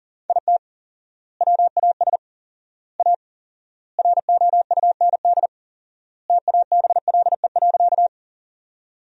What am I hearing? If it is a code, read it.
Morse code, heard as a keyed tone, about 40 wpm, 700 Hz. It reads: IT WAS A ROUND TABLE.
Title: 4x Word-Spacing - Sentences from Top 400 Words - 40wpm